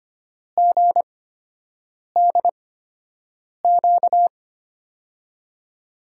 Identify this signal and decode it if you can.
Morse code, heard as a keyed tone, about 25 wpm, spaced as for 7 wpm, 700 Hz. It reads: GDQ
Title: USA Call Signs - 6 Characters - 8x Character Spacing - 25wpm